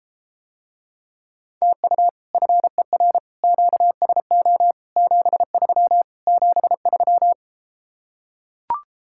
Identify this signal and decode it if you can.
Morse code, heard as a keyed tone, about 33 wpm, 700 Hz. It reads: TU FER QSO 73 73